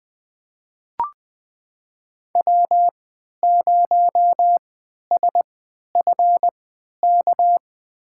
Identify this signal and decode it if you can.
Morse code, heard as a keyed tone, about 20 wpm, spaced as for 11 wpm, 700 Hz. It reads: W0SFK
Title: USA Call Signs - 5 Characters - 3x Character Spacing - 20wpm